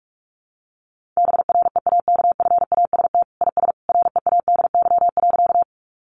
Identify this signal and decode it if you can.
Morse code, heard as a keyed tone, about 44 wpm, 700 Hz. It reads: BREAKFAST IS READY.
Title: Sentences from Top 1500 Words 45wpm